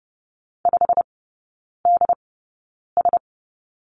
Morse code, heard as a keyed tone, about 30 wpm, spaced as for 9 wpm, 690 Hz. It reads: 5DS
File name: Lesson 17e - Letter u - Three Characters 30wpm